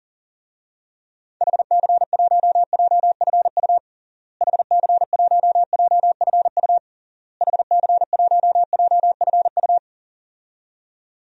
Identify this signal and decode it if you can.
Morse code, heard as a keyed tone, about 40 wpm, 700 Hz. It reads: HC1JFU HC1JFU HC1JFU
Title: DXCC-100 Call Signs - 6 Characters - Repeated 3x - 40wpm